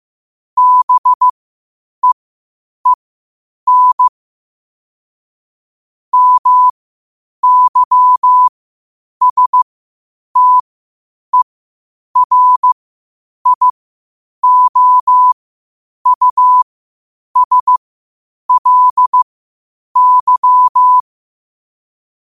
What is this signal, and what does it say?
Morse code, heard as a keyed tone, about 15 wpm, spaced as for 8 wpm, 1000 Hz. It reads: BEEN MYSTERIOUSLY
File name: qotdfiles / 8wpm